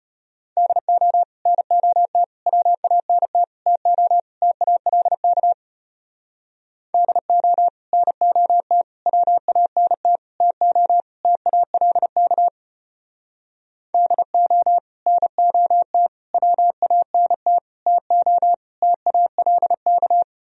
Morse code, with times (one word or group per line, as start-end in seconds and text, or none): 0.6-1.2 DO
1.5-2.2 NOT
2.5-3.4 WANT
3.7-4.2 TO
4.4-5.5 TALK
6.9-7.7 DO
7.9-8.8 NOT
9.1-10.2 WANT
10.4-11.0 TO
11.2-12.5 TALK
13.9-14.8 DO
15.1-16.1 NOT
16.3-17.6 WANT
17.9-18.5 TO
18.8-20.2 TALK